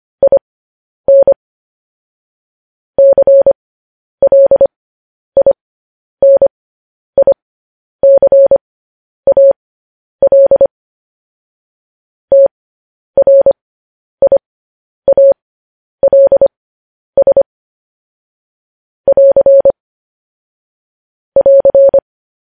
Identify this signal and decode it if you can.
Morse code, heard as a keyed tone, about 25 wpm, spaced as for 10 wpm, 550 Hz. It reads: IN CLINICAL TRIALS <AR> <AR>
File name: podcast / News headlines-1x-25x10